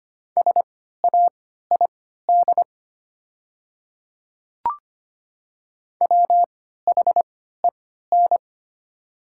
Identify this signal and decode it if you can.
Morse code, heard as a keyed tone, about 25 wpm, spaced as for 14 wpm, 700 Hz. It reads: SAID WHEN
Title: 3x Character Spacing - Top 100 Words - 25wpm